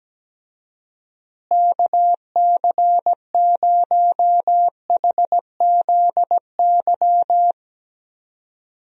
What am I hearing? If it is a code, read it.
Morse code, heard as a keyed tone, about 17 wpm, 700 Hz. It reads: KC0HZY
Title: USA Call Signs - 6 Characters - Repeated 2x - 17wpm